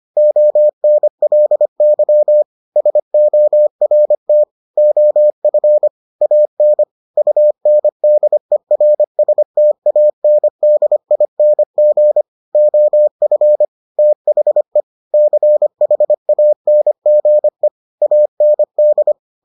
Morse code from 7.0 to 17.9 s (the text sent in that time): UNDERSTANDING OF THE CHANGE